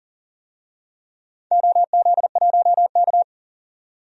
Morse code, heard as a keyed tone, about 40 wpm, 700 Hz. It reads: OZ1K
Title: DXCC-100 Call Signs - 4 Characters - 40wpm